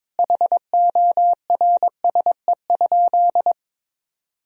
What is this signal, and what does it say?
Morse code, heard as a keyed tone, about 22 wpm, 700 Hz. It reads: HORSE?
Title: Sentences from Top 1000 Words - Review Part 2 - 22wpm